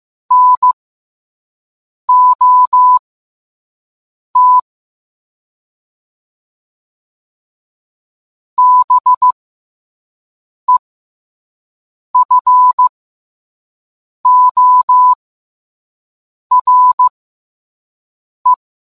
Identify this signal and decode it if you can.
Morse code, heard as a keyed tone, about 15 wpm, spaced as for 5 wpm, 1000 Hz. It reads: NOT BEFORE